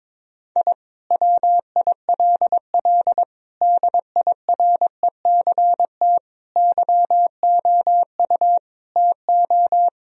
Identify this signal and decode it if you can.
Morse code, heard as a keyed tone, about 22 wpm, 700 Hz. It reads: I WILL DIRECT YOU TO
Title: Sentences from Top 1000 Words - Review Part 2 - 22wpm